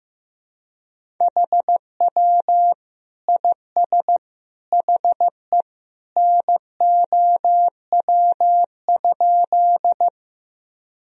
Morse code, heard as a keyed tone, about 16 wpm, 700 Hz. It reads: HW IS HE NOW?
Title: Sentences from Top 100 Words and Abbreviations 15wpm